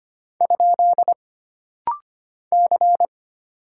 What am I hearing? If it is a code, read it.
Morse code, heard as a keyed tone, about 25 wpm, 700 Hz. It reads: ? C
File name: Single Character (Letter:Number:Punctuation:Prosign) - Rapid-Fire - 25wpm